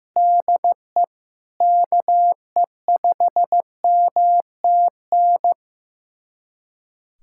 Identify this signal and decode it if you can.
Morse code, heard as a keyed tone, about 15 wpm, 700 Hz. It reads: DE KE5MTN
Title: USA Callsign DE Callsign - 6 Characters - 15wpm